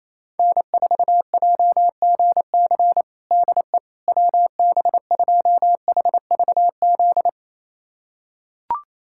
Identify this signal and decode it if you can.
Morse code, heard as a keyed tone, about 28 wpm, 700 Hz. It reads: N4JGC DE WB2HVZ
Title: USA Callsign DE Callsign 28wpm